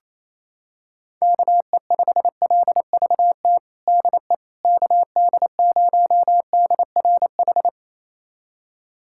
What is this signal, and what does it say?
Morse code, heard as a keyed tone, about 28 wpm, 700 Hz. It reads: KE5LVT DE KD0DRH
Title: USA Callsign DE Callsign - 6 Characters - 28wpm